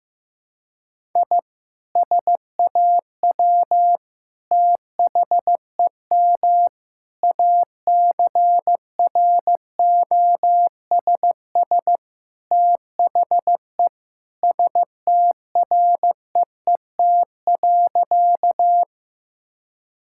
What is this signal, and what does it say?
Morse code, heard as a keyed tone, about 15 wpm, 700 Hz. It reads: I SAW THEM ACROSS THE STREET.
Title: Sentences from Top 1000 Words - Review Part 5 - 15wpm